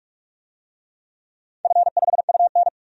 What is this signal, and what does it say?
Morse code, heard as a keyed tone, about 46 wpm, 700 Hz. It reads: F5UN
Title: DXCC-100 Call Signs - 4 Characters - 45wpm